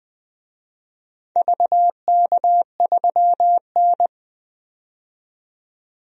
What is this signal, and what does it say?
Morse code, heard as a keyed tone, about 20 wpm, 700 Hz. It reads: VK3N
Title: DXCC-100 Call Signs - 4 Characters - 20wpm